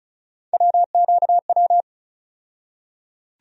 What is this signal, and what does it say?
Morse code, heard as a keyed tone, about 34 wpm, 700 Hz. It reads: WQW
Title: USA Call Sign Postfixes - 3 Letters 35wpm